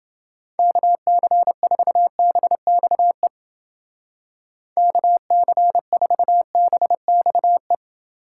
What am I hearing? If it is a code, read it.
Morse code, heard as a keyed tone, about 28 wpm, 700 Hz. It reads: KC4BXE KC4BXE